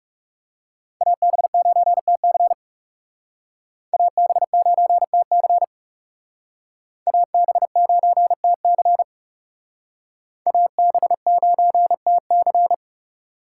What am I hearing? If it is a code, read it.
Morse code, heard as a keyed tone, about 38 wpm, 700 Hz. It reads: AB9TC AB9TC AB9TC AB9TC